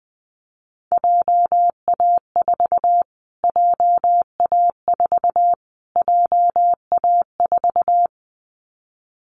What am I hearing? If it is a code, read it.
Morse code, heard as a keyed tone, about 20 wpm, 700 Hz. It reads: JA4 JA4 JA4
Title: Callsign Prefixes for Top 25 DXCC - Repeated 3x - 20wpm